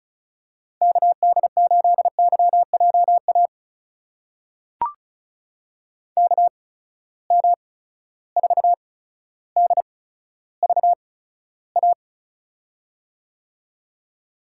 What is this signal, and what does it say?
Morse code, heard as a keyed tone, about 36 wpm, spaced as for 15 wpm, 700 Hz. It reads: KD8YJA KM4DVA
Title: USA Call Signs - 6 Characters - 8x Character Spacing - 35wpm